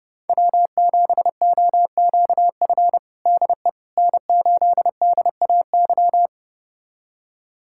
Morse code, heard as a keyed tone, about 30 wpm, 700 Hz. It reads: W7OQF DE N8DAY
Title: USA Callsign DE Callsign - 5 Characters - 30wpm